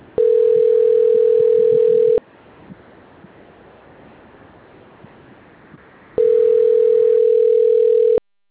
Ringback tone (North America), 440 + 480 Hz.
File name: rg-1600-unknown- - - .409